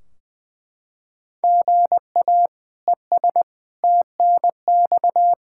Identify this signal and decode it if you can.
Morse code, heard as a keyed tone, about 21 wpm, 700 Hz. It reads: GA ES TNX
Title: Practice QSO - Exchange 1 - Beginning - 20wpm